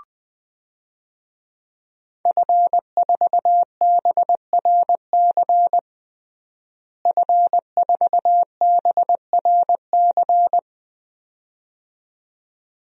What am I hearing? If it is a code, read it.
Morse code, heard as a keyed tone, about 20 wpm, 700 Hz. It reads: F4BRC F4BRC
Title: DXCC-100 Call Signs - 5 Characters - Repeated 2x - 20wpm